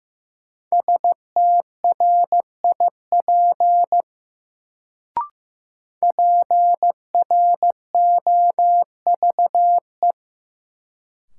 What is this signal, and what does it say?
Morse code, heard as a keyed tone, about 15 wpm, 700 Hz. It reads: STRIP PROVE
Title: Sentences from Top 1600 Words - New Word Drill - 15wpm